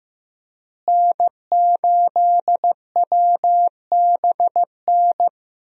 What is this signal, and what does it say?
Morse code, heard as a keyed tone, about 15 wpm, 700 Hz. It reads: N8WBN
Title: USA Call Signs - 5 Characters - Repeated 2x - 15wpm